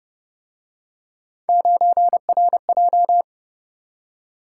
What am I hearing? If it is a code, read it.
Morse code, heard as a keyed tone, about 28 wpm, 700 Hz. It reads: 9RJ